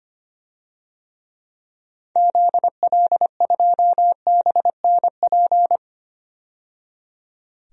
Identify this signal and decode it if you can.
Morse code, heard as a keyed tone, about 25 wpm, 700 Hz. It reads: ZL2BNP
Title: DXCC-100 Call Signs - 6 Characters - 25wpm